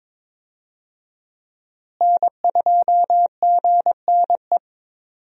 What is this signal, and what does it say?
Morse code, heard as a keyed tone, about 22 wpm, 700 Hz. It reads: N2GNE
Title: USA Call Signs - 5 Characters - Repeated 3x - 22wpm